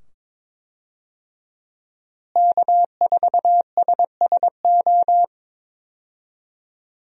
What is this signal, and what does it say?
Morse code, heard as a keyed tone, about 21 wpm, 700 Hz. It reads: K4SSO
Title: USA Call Signs - 5 Characters - Repeated 2x - 22wpm